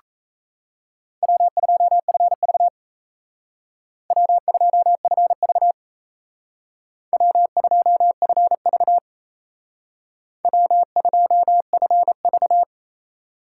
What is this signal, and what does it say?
Morse code, heard as a keyed tone, about 35 wpm, 700 Hz. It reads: W2FV W2FV W2FV W2FV